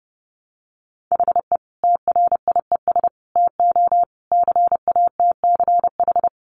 Morse code, heard as a keyed tone, about 30 wpm, 700 Hz. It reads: HE TRIES TO CATCH
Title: Additional Easy Phrases 30wpm